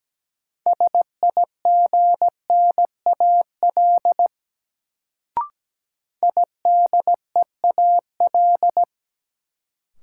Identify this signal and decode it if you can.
Morse code, heard as a keyed tone, about 17 wpm, 700 Hz. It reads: SIGNAL IDEAL
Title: Sentences from Top 1400 Words - New Word Drill Alt - 17wpm